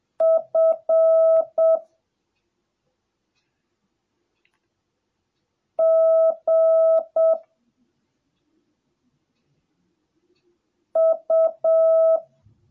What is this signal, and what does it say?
Morse code, heard as a keyed tone, about 7 wpm, 650 Hz. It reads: F G U